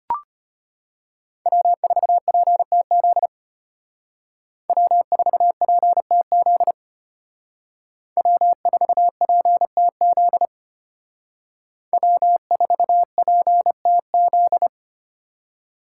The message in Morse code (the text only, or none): W4PTZ W4PTZ W4PTZ W4PTZ